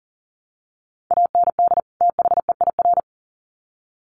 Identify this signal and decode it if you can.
Morse code, heard as a keyed tone, about 40 wpm, 700 Hz. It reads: AND THEIR
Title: Sets of 2 Words 40wpm